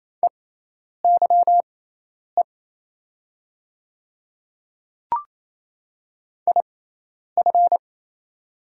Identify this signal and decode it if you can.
Morse code, heard as a keyed tone, about 27 wpm, spaced as for 10 wpm, 700 Hz. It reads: EYE IF
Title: 6x Character Spacing - Top 500 Words - 28wpm